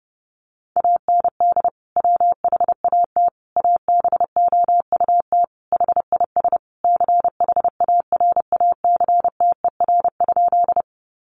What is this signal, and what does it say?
Morse code, heard as a keyed tone, about 30 wpm, 700 Hz. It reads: AND WHAT ABOUT HIS CHARACTER?
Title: Sentences from Top 1200 Words 30wpm